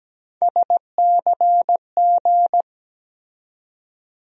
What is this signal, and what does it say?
Morse code, heard as a keyed tone, about 17 wpm, 700 Hz. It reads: SCG
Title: USA Call Sign Postfixes - 3 Letters 17wpm